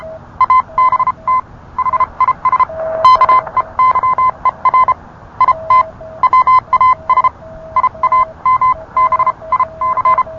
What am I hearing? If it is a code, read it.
Morse code, heard as a keyed tone, about 31 wpm, spaced as for 26 wpm, 1030 Hz. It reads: ABT HIS KEYER IT WAS IAMBIC